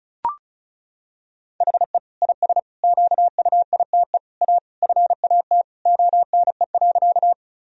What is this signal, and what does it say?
Morse code, heard as a keyed tone, about 34 wpm, 700 Hz. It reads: HE IS QUITE A FAT ONE.
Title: Sentences from Top 1000 Words - Review Part 4 - 35wpm